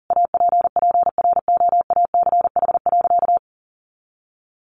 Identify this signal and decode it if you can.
Morse code, heard as a keyed tone, about 40 wpm, 700 Hz. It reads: APPROACH.
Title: Sentences from Top 1400 Words 40wpm